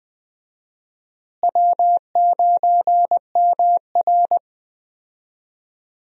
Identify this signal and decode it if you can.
Morse code, heard as a keyed tone, about 20 wpm, 700 Hz. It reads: W9MR
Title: USA Call Signs - 4 Characters - Repeated 3x - 20wpm